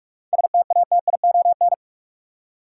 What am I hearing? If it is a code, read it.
Morse code, heard as a keyed tone, about 46 wpm, 700 Hz. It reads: STATION
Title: Seven Letter Words - 45wpm